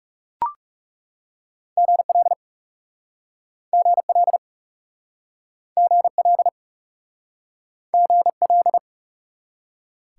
Morse code, heard as a keyed tone, about 38 wpm, 700 Hz. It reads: GL GL GL GL